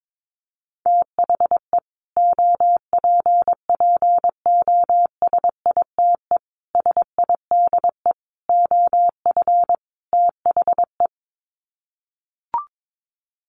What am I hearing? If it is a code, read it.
Morse code, heard as a keyed tone, about 22 wpm, 700 Hz. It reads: THE OPPOSITE SIDE OF THE
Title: Sets of 5 Words Encore 22wpm